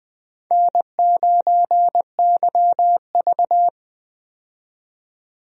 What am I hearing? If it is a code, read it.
Morse code, heard as a keyed tone, about 20 wpm, 700 Hz. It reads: N9YV
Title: USA Call Signs - 4 Characters - 20wpm